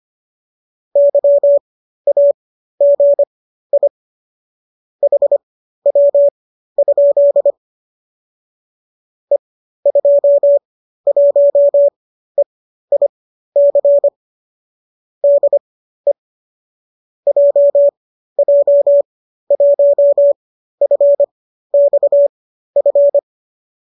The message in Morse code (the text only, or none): YAGI HW? E21EIC DE JJ1FXF